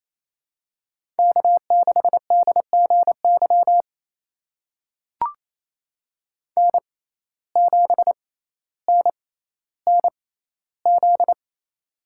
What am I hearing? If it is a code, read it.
Morse code, heard as a keyed tone, about 28 wpm, spaced as for 14 wpm, 700 Hz. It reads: K6DGY N7NNZ
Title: USA Call Signs - 5 Characters - 6x Character Spacing - 28wpm